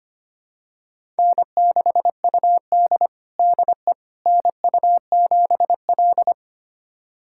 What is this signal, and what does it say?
Morse code, heard as a keyed tone, about 25 wpm, 700 Hz. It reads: N6UD DE NU7L
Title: USA Callsign DE Callsign - 4 Characters - 25wpm